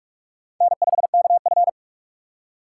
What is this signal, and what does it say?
Morse code, heard as a keyed tone, about 46 wpm, 700 Hz. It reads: N5KF
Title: USA Call Signs - 4 Characters - 3x Character Spacing - 45wpm